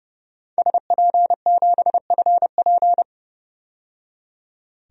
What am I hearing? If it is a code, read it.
Morse code, heard as a keyed tone, about 30 wpm, 700 Hz. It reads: SP7FP
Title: DXCC-100 Call Signs - 5 Characters - 30wpm